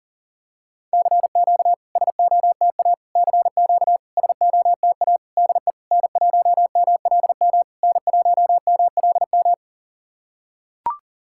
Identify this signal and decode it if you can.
Morse code, heard as a keyed tone, about 40 wpm, 690 Hz. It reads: CQ SOTA CQ SOTA DE N1MLM N1MLM
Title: SOTA QSO Practice 40wpm